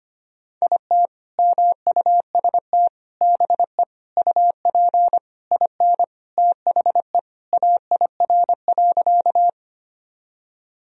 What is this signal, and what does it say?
Morse code, heard as a keyed tone, about 25 wpm, 700 Hz. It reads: IT MUST BE UP IN THE AIR.
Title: Sentences from Top 1000 Words - Review Part 1 - 25wpm